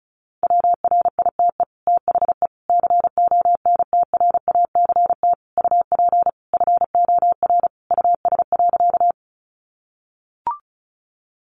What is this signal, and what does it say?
Morse code, heard as a keyed tone, about 34 wpm, 700 Hz. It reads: WRITE THE CONTRACT UP FOR US.
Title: Sentences from Top 1400 Words 35wpm